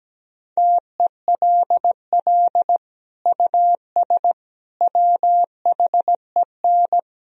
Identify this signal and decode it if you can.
Morse code, heard as a keyed tone, about 17 wpm, 700 Hz. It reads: TELL US WHEN
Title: Sentences from Top 1000 Words - Review Part 2 - 17wpm